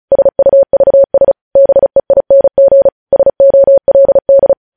Morse code, heard as a keyed tone, about 36 wpm, 550 Hz. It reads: SUVS BEING SOLD
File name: podcast / News headlines-1x-35x35